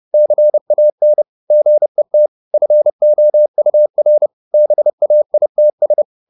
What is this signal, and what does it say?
Morse code, heard as a keyed tone, about 30 wpm, 600 Hz. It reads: CAN GET FOUR BAITS
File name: the old man and the sea / the old man and the sea 1 30wpm